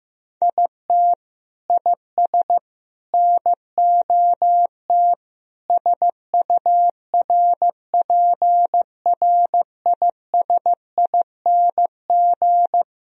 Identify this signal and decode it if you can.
Morse code, heard as a keyed tone, about 15 wpm, 700 Hz. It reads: IT IS NOT SURPRISING
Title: Speed-Racing - Sets of 5 Words 15wpm